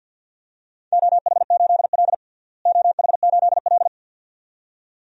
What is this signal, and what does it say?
Morse code, heard as a keyed tone, about 48 wpm, 700 Hz. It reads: OH8L OH8L
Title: DXCC-100 Call Signs - 4 Characters - Repeated 2x - 50wpm